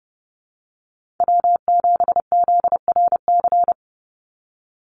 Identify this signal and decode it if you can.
Morse code, heard as a keyed tone, about 30 wpm, 700 Hz. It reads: W7ZRC
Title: Call Signs Repeated 3x - 30wpm